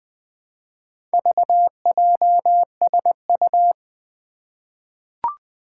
Morse code, heard as a keyed tone, about 20 wpm, 700 Hz. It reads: VJSU
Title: Four Challenging Letter-Number Combinations 20wpm